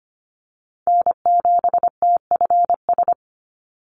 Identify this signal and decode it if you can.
Morse code, heard as a keyed tone, about 25 wpm, 700 Hz. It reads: N7TFS
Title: USA Call Signs 25wpm